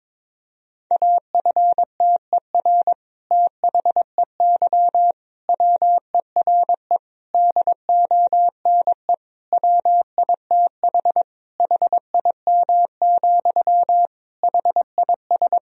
Morse code, heard as a keyed tone, about 22 wpm, 700 Hz. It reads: AFTER THEY WERE DONE WITH HIM, HIS
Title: Sentences from Top 1000 Words - Review Part 2 - 22wpm